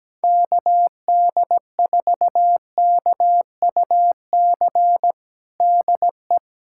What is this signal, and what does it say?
Morse code, heard as a keyed tone, about 17 wpm, 700 Hz. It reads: KD4KUC DE
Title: USA Callsign DE Callsign 17wpm